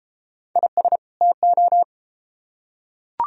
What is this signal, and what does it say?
Morse code, heard as a keyed tone, about 33 wpm, 700 Hz. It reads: IS TO